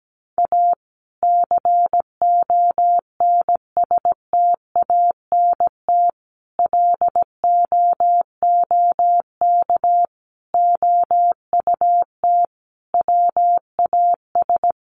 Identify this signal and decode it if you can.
Morse code, heard as a keyed tone, about 17 wpm, 700 Hz. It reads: A CONSTANT LOOK OUT WAS
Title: Sentences from Top 1500 Words 17wpm